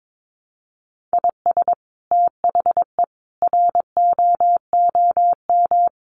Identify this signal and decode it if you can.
Morse code, heard as a keyed tone, about 22 wpm, 700 Hz. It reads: IS THE ROOM
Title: Sentences from Top 1400 Words 22wpm